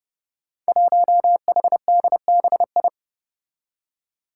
Morse code, heard as a keyed tone, about 30 wpm, 700 Hz. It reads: 1HDBI
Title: Five Challenging Letter-Number Combinations 30wpm